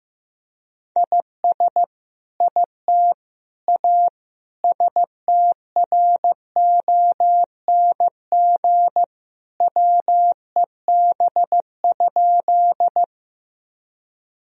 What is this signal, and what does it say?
Morse code, heard as a keyed tone, about 16 wpm, 700 Hz. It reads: IS IT A STRONG WEB?
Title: Sentences from Top 1000 Words - Review Part 4 - 15wpm